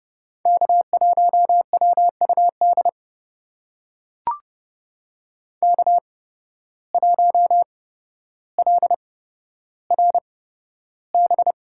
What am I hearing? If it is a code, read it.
Morse code, heard as a keyed tone, about 31 wpm, spaced as for 13 wpm, 700 Hz. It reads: K1WUD K1LRB
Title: USA Call Signs - 5 Characters - 8x Character Spacing - 30wpm